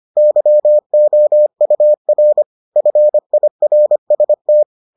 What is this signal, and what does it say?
Morse code, heard as a keyed tone, about 25 wpm, 600 Hz. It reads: YOUR FIRST